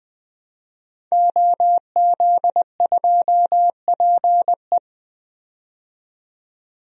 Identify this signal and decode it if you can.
Morse code, heard as a keyed tone, about 21 wpm, 700 Hz. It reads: OZ2PE
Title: DXCC-100 Call Signs - 5 Characters - 20wpm